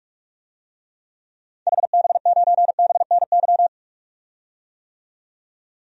Morse code, heard as a keyed tone, about 41 wpm, 700 Hz. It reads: HB9BNY
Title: DXCC-100 Call Signs - 6 Characters - 45wpm